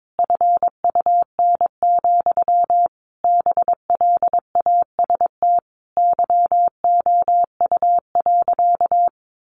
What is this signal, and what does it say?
Morse code, heard as a keyed tone, about 22 wpm, 700 Hz. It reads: FUN, BLAST YOU.
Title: A Sound Of Thunder 22wpm